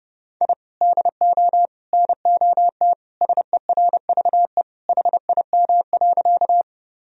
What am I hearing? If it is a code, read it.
Morse code, heard as a keyed tone, about 29 wpm, 700 Hz. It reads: I DO NOT SERVE HIM.
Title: Sentences from Top 1000 Words - Review Part 2 - 30wpm